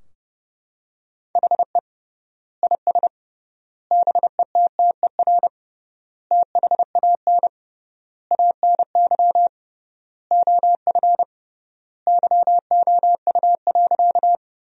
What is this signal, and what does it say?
Morse code, heard as a keyed tone, about 32 wpm, 700 Hz. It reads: HE IS BETTER THAN ANY OF YOU.
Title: 3x Word-Spacing - Sentences from Top 400 Words - 30wpm